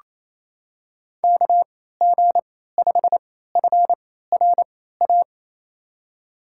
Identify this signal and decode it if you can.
Morse code, heard as a keyed tone, about 28 wpm, spaced as for 16 wpm, 700 Hz. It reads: KG5FRA
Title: USA Call Signs - 3x Character Spacing - 28wpm